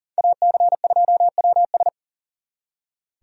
Morse code, heard as a keyed tone, about 50 wpm, 700 Hz. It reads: AC2WS